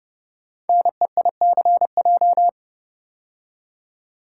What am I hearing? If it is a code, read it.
Morse code, heard as a keyed tone, about 30 wpm, 700 Hz. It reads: NEICJ
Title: Code Groups - 5 Letters - 30wpm